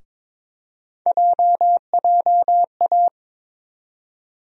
Morse code, heard as a keyed tone, about 22 wpm, 700 Hz. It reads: JJA
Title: USA Call Sign Postfixes - 3 Letters 22wpm